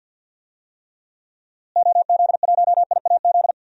Morse code, heard as a keyed tone, about 47 wpm, 700 Hz. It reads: OZ1IAZ
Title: DXCC-100 Call Signs - 6 Characters - Repeated 2x - 50wpm